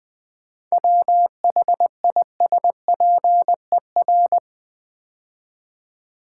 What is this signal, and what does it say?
Morse code, heard as a keyed tone, about 20 wpm, 700 Hz. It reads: WHISPER
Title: 7 Letter Words - First 3 Letters Given - 20wpm